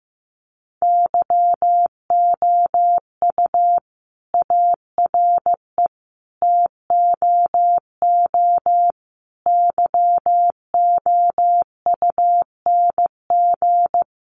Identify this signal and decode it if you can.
Morse code, heard as a keyed tone, about 15 wpm, 700 Hz. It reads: YOU ARE TOO YOUNG
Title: Sentences from Top 1400 Words 15wpm